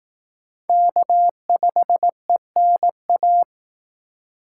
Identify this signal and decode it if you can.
Morse code, heard as a keyed tone, about 18 wpm, 700 Hz. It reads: K5ENA
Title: Speed Racing - USA 1x3 Call Signs 15wpm